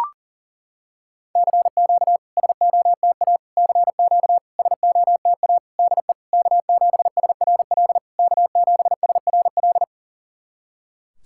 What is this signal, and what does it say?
Morse code, heard as a keyed tone, about 40 wpm, 690 Hz. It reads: CQ SOTA CQ SOTA DE K7SRL K7SRL
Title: SOTA QSO Practice 40wpm